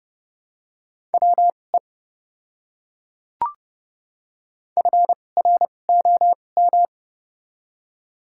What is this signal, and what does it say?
Morse code, heard as a keyed tone, about 30 wpm, spaced as for 22 wpm, 700 Hz. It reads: WE FROM
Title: 2x Character Spacing - Top 100 Words - 30wpm